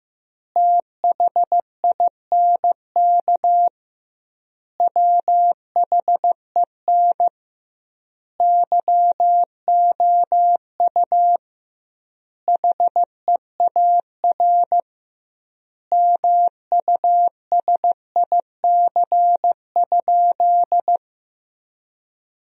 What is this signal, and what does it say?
Morse code, heard as a keyed tone, about 15 wpm, 700 Hz. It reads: THINK WHEN YOU HEAR MUSIC?
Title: 2x Word-Spacing - Sentences from Top 300 Words - 15wpm